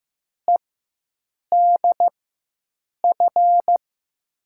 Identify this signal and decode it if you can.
Morse code, heard as a keyed tone, about 15 wpm, spaced as for 7 wpm, 700 Hz. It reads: EDF
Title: USA Call Signs - 6 Characters - 4x Character Spacing - 15wpm